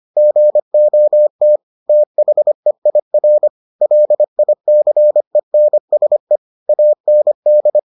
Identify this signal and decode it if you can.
Morse code, heard as a keyed tone, about 25 wpm, 600 Hz. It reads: GOT THEIR LICENSE AND